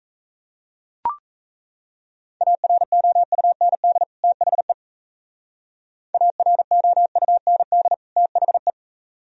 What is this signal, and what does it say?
Morse code, heard as a keyed tone, about 36 wpm, 700 Hz. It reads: AROUND THE AROUND THE